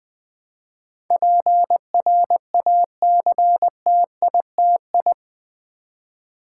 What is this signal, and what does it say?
Morse code, heard as a keyed tone, about 20 wpm, 700 Hz. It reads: PRACTITI
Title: Build A Word - 4 Letters At A Time - 12-Letter Words - 20wpm